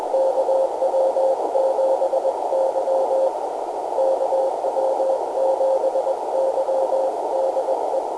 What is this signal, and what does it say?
Morse code, heard as a keyed tone, about 21 wpm, 530 Hz, with noise. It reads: KW7Y KW7Y DE
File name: camano / g3fpq-topband